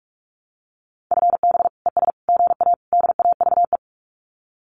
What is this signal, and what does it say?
Morse code, heard as a keyed tone, about 44 wpm, 700 Hz. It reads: FB ES GA DAVE
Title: Common QSO Phrases Encore 45wpm